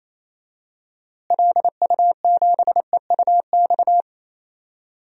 Morse code, heard as a keyed tone, about 28 wpm, 700 Hz. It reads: LU7EUX
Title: DXCC-100 Call Signs - 6 Characters - Repeated 3x - 28wpm